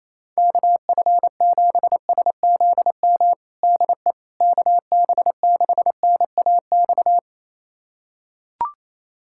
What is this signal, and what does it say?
Morse code, heard as a keyed tone, about 28 wpm, 700 Hz. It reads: KF7SZM DE KB6NAX
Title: USA Callsign DE Callsign - 6 Characters - 28wpm